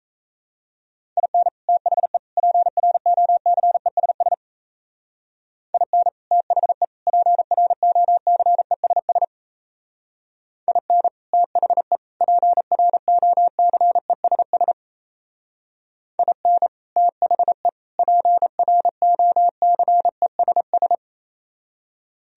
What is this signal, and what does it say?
Morse code, heard as a keyed tone, about 35 wpm, 700 Hz. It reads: IN THE PROCESS IN THE PROCESS IN THE PROCESS IN THE PROCESS
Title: Speed-Racing - Sets of 3 Words 28wpm